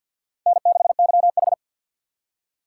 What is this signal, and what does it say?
Morse code, heard as a keyed tone, about 64 wpm, 700 Hz. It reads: N6YH